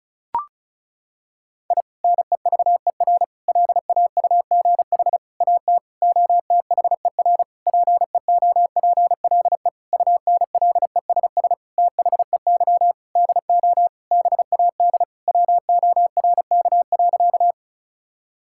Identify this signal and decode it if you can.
Morse code, heard as a keyed tone, about 34 wpm, 700 Hz. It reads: I NEVER LAUGH AT OTHER PEOPLE UNLESS THEY DO BAD WORK.
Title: Sentences from Top 1000 Words - Review Part 6 - 35wpm